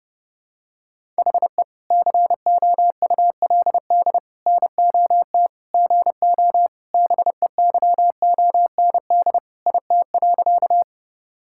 Morse code, heard as a keyed tone, about 30 wpm, 700 Hz. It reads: HE COULD NOT GO BEYOND IT.
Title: Sentences from Top 1000 Words - Review Part 6 - 30wpm